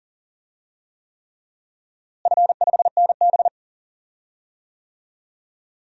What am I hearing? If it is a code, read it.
Morse code, heard as a keyed tone, about 40 wpm, 700 Hz. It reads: F5NB
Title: DXCC-100 Call Signs - 4 Characters - Repeated 3x - 40wpm